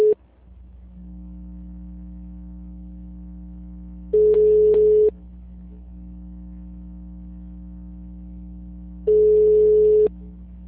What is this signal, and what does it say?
Ringback tone (Europe, 425 Hz).